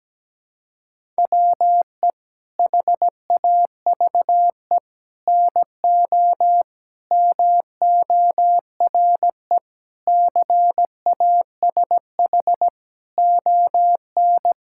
Morse code, heard as a keyed tone, about 17 wpm, 700 Hz. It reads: WE HAVE NO MORE CASH ON
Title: Sentences from Top 1000 Words - Review Part 5 - 17wpm